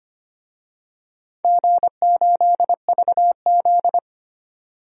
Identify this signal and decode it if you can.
Morse code, heard as a keyed tone, about 25 wpm, 700 Hz. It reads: G8VZ
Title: DXCC-100 Call Signs - 4 Characters - 25wpm